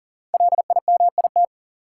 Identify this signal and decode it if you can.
Morse code, heard as a keyed tone, about 38 wpm, 700 Hz. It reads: LIMIT
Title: Five Letter Words - 40wpm